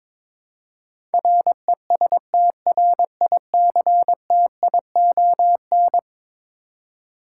Morse code, heard as a keyed tone, about 22 wpm, 700 Hz. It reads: RESTRICTION
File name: practice / 22wpm / Eleven Letter Words - 22wpm